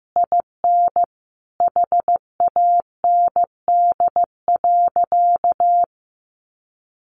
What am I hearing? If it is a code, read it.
Morse code, heard as a keyed tone, about 15 wpm, 700 Hz. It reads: IN HAND.
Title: Sentences from Top 1300 Words 15wpm